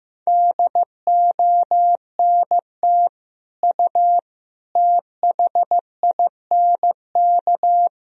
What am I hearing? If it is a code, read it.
Morse code, heard as a keyed tone, about 15 wpm, 700 Hz. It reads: DONT U THINK
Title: Sentences from Top 300 Words and Abbreviations 15wpm